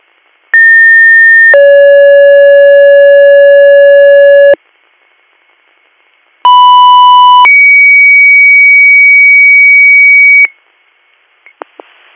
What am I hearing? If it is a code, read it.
Two-tone page, heard as a one-second tone, then a three-second tone, about 1800 then 560 Hz, about 990 then 2250 Hz. It is heard from 0.5 to 10.5 s.